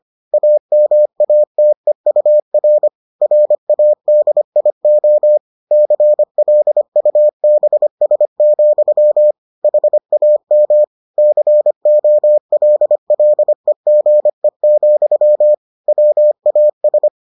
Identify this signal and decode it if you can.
Morse code, heard as a keyed tone, about 25 wpm, 600 Hz. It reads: AMATEUR RADIO CLUBS, HAM COLLEGE, WAS